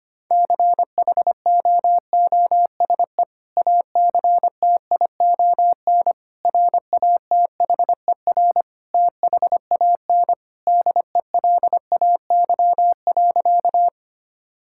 Morse code, heard as a keyed tone, about 25 wpm, 700 Hz. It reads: CHOOSE ACTION RATHER THAN DELAY.
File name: practice / 25wpm / Sentences with 2-Word Phrases - 25wpm